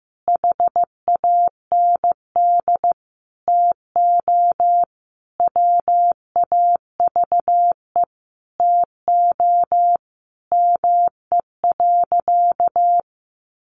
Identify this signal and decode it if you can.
Morse code, heard as a keyed tone, about 15 wpm, 700 Hz. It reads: HAND TO WAVE TO ME.
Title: Sentences from Top 1400 Words 15wpm